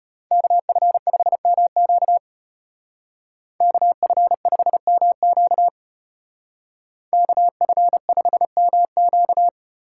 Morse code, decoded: KF5MQ KF5MQ KF5MQ

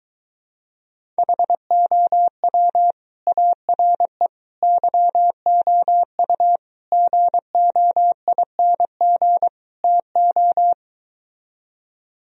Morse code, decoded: HOW ARE YOU GOING TO

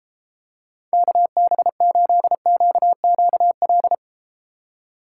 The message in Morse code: KB8QQL

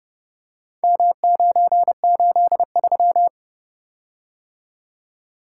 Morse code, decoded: M983

